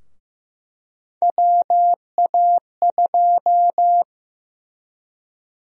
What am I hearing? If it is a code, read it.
Morse code, heard as a keyed tone, about 15 wpm, 700 Hz. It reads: WA2
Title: USA Call Sign Prefixes - Repeated 2x - 15wpm